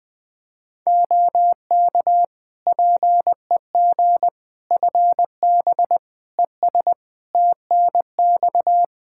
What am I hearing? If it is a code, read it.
Morse code, heard as a keyed tone, about 20 wpm, 700 Hz. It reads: OK PEG FB ES TNX
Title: Practice QSO - Exchange 2 - Info - 20wpm